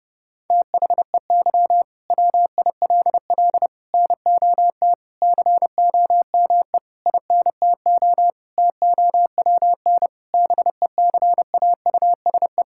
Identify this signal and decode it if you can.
Morse code, heard as a keyed tone, about 32 wpm, 700 Hz. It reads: THEY WILL NOT COME INTO TOWN BECAUSE
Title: Sentences from Top 1000 Words - Review Part 5 - 30wpm